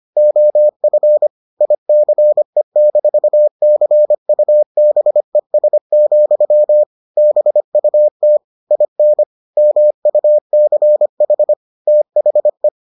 Morse code, heard as a keyed tone, about 25 wpm, 600 Hz. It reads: OF ICE-CUBES, BUT IN MUCH THE